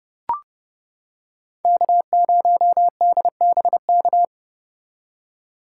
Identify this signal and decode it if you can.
Morse code, heard as a keyed tone, about 31 wpm, 700 Hz. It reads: K0DBK